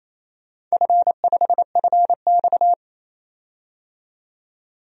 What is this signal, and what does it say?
Morse code, heard as a keyed tone, about 28 wpm, 700 Hz. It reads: F5FX
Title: DXCC-100 Call Signs - 4 Characters - 28wpm